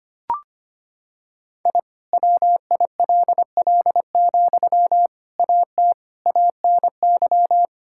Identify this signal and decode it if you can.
Morse code, heard as a keyed tone, about 25 wpm, 700 Hz. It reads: I WILL, AT ANY
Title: Sentences from Top 1000 Words - Review Part 5 - 25wpm